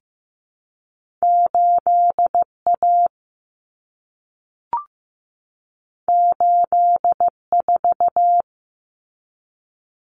Morse code, heard as a keyed tone, about 15 wpm, 700 Hz. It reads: 8A 84
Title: Common Cut Numbers - 2-Digit 15wpm